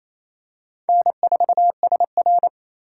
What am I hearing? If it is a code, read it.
Morse code, heard as a keyed tone, about 27 wpm, 700 Hz. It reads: N4SR
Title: K1USN SST QSO Practice - 28wpm